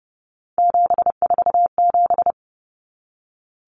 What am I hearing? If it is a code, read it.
Morse code, heard as a keyed tone, about 30 wpm, 700 Hz. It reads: 747
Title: Common Cut Numbers - 3-Digit 30wpm